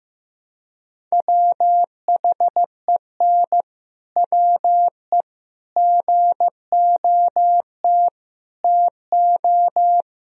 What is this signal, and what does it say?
Morse code, heard as a keyed tone, about 15 wpm, 700 Hz. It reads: WHEN WE GOT TO
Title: Sentences from Top 1000 Words - Review Part 1 - 15wpm